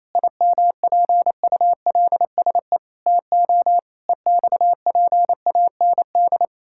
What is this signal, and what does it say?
Morse code, heard as a keyed tone, about 28 wpm, 700 Hz. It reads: IMPULSE TO EXPAND
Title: Pangram Challenge v2 - 28wpm